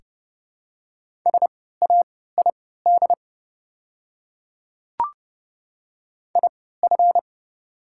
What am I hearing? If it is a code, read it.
Morse code, heard as a keyed tone, about 30 wpm, spaced as for 17 wpm, 700 Hz. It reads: SAID IF
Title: 3x Character Spacing - Top 100 Words - 30wpm